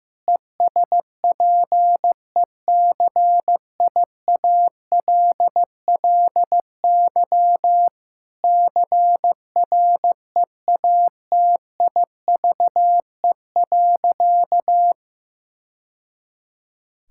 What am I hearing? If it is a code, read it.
Morse code, heard as a keyed tone, about 16 wpm, 700 Hz. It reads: ESPECIALLY CREATIVE.
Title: Sentences from Top 1000 Words - Review Part 5 - 15wpm